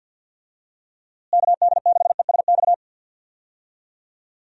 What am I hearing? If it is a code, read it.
Morse code, heard as a keyed tone, about 50 wpm, 700 Hz. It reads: KD6ESX
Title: USA Call Signs - 6 Characters - Repeated 2x - 50wpm